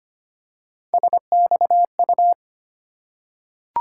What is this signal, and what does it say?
Morse code, heard as a keyed tone, about 25 wpm, 700 Hz. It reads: SXU
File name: Three Letter-Number Combinations - 25wpm